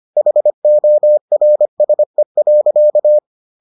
Morse code, heard as a keyed tone, about 25 wpm, 600 Hz. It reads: HORSE.